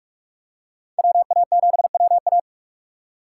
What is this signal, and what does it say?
Morse code, heard as a keyed tone, about 43 wpm, 700 Hz. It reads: WA7WA
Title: USA Call Signs - 5 Characters - 4x Character Spacing - 45wpm